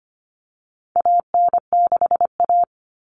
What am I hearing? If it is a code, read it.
Morse code, heard as a keyed tone, about 25 wpm, 700 Hz. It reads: AN6A
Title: Common Cut Numbers - 4-Digit 25wpm